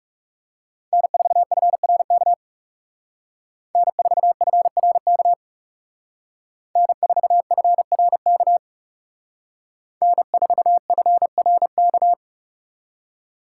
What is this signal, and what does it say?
Morse code, heard as a keyed tone, about 36 wpm, 700 Hz. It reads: N4FRK N4FRK N4FRK N4FRK
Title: Speed Racing - USA 1x3 Call Signs 30wpm